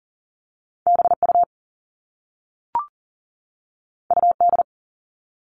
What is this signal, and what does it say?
Morse code, heard as a keyed tone, about 40 wpm, 700 Hz. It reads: BU UD
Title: All Cut Numbers - 2-Digit 40wpm